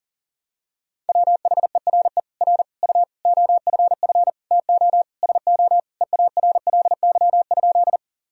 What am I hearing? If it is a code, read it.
Morse code, heard as a keyed tone, about 37 wpm, 700 Hz. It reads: WHERE R U OFF TO SO EARLY?
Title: Sentences from Top 400 Words and Abbreviations 40wpm